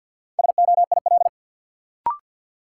Morse code, heard as a keyed tone, about 52 wpm, 700 Hz. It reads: SOIL
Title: Sentences from Top 1500 Words - New Word Drill Alt - 50wpm